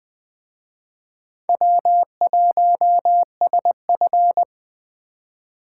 Morse code, heard as a keyed tone, about 24 wpm, 700 Hz. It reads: W1SF
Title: Speed Racing - USA 1x2 Call Signs 20wpm